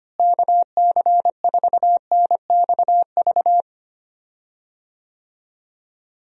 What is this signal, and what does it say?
Morse code, heard as a keyed tone, about 25 wpm, 700 Hz. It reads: KC4NXV
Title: USA Call Signs - 6 Characters - Repeated 3x - 25wpm